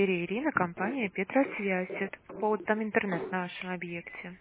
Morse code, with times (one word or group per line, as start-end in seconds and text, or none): none